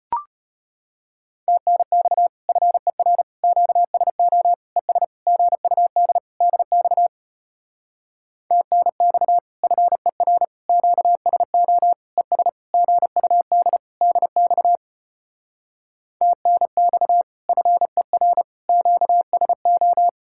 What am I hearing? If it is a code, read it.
Morse code, heard as a keyed tone, about 33 wpm, 700 Hz. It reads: TNX FER QSO ES GUD DX TNX FER QSO ES GUD DX TNX FER QSO